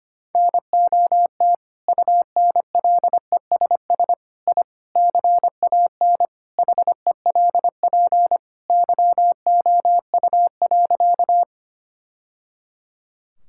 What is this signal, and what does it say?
Morse code, heard as a keyed tone, about 25 wpm, 700 Hz. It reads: NOT UNLESS I CAN HELP YOU.
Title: Sentences from Top 1000 Words - Review Part 6 - 25wpm